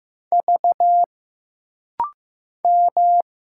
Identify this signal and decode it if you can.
Morse code, heard as a keyed tone, about 15 wpm, 700 Hz. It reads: V M
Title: Single Letters - Rapid-Fire 15wpm